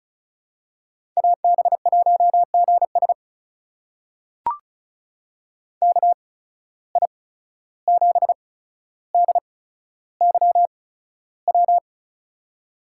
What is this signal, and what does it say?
Morse code, heard as a keyed tone, about 36 wpm, spaced as for 14 wpm, 700 Hz. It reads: AB1GS KI7DYW
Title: USA Call Signs - 8x Character Spacing - 35wpm